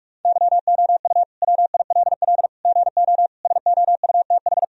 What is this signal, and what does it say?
Morse code, heard as a keyed tone, about 46 wpm, 700 Hz. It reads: YOU WILL GO SOUTH
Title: Sentences from Top 1000 Words - Review Part 2 - 45wpm